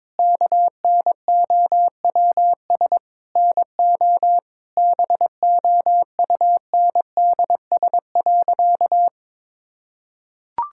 Morse code, heard as a keyed tone, about 22 wpm, 700 Hz. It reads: KNOWS NO BOUNDS.
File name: Difficult Sentences with Hints - 22wpm